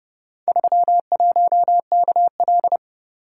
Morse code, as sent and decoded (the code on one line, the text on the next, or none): ...-- .---- -.- .-..
31KL